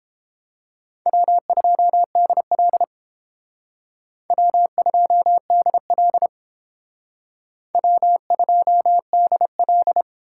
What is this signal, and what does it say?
Morse code, heard as a keyed tone, about 29 wpm, 700 Hz. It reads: W2DL W2DL W2DL